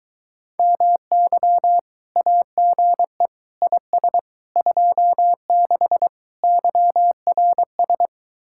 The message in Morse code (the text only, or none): MY AGE IS 26 YRS